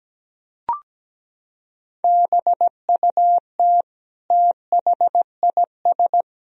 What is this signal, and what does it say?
Morse code, heard as a keyed tone, about 17 wpm, 700 Hz. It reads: BUT THIS
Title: Sentences from Top 1000 Words - Review Part 3 - 17wpm